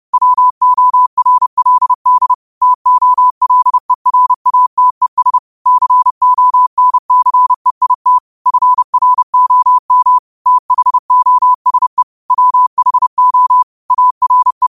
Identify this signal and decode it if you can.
Morse code, heard as a keyed tone, about 30 wpm, 1000 Hz. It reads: WORLD TOLERATES CONCEIT FROM THOSE WHO ARE
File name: qotdfiles / 30wpm